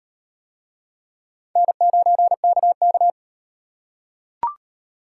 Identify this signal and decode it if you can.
Morse code, heard as a keyed tone, about 37 wpm, 700 Hz. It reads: N9KK